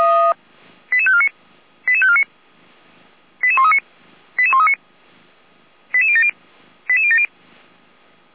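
Five-tone selective call (ZVEI), heard as quick runs of five tones, tones of about 70 ms. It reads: 88549 88549 88139 88139 88070 88070
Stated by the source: ZVEI five-tone selective call received on SDR.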